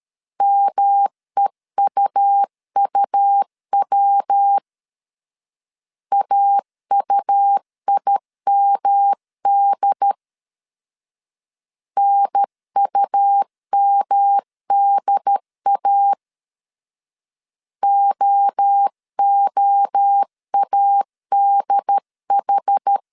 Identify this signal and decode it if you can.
Morse code, heard as a keyed tone, about 13 wpm, 800 Hz. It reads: MEUUW AUIMD NUMDA OOADH
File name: CW / DET / 1 3 1500C S A 12wpm 2905